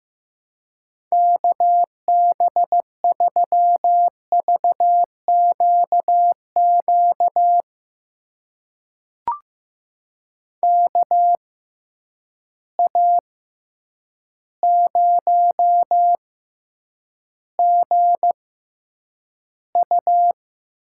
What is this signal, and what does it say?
Morse code, heard as a keyed tone, about 15 wpm, spaced as for 8 wpm, 700 Hz. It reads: KB3VQQ KA0GU
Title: USA Call Signs - 6 Characters - 6x Character Spacing - 15wpm